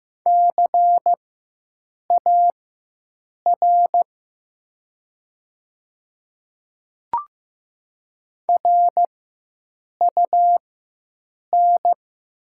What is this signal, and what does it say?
Morse code, heard as a keyed tone, about 15 wpm, spaced as for 7 wpm, 700 Hz. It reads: CAR RUN
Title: 4x Character Spacing - Top 300 Words - 15wpm